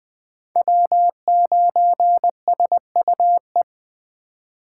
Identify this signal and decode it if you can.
Morse code, heard as a keyed tone, about 19 wpm, 700 Hz. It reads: W9SUE